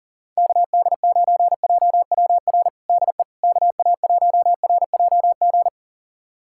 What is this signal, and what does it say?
Morse code, heard as a keyed tone, about 40 wpm, 700 Hz. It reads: KD9JWR DE KA1RJG
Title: USA Callsign DE Callsign 40wpm